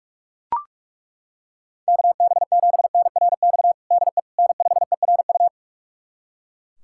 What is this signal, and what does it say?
Morse code, heard as a keyed tone, about 41 wpm, 700 Hz. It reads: KB7NRX DE N5ERU
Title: USA Callsign DE Callsign 45wpm